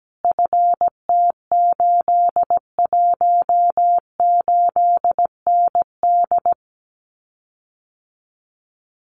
Morse code, heard as a keyed tone, about 17 wpm, 700 Hz. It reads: FT818ND
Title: Common QSO Phrases Encore 17wpm